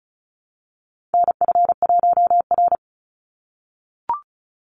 Morse code, heard as a keyed tone, about 34 wpm, 700 Hz. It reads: NF1R
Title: NA SOTA Chaser Callsigns 35wpm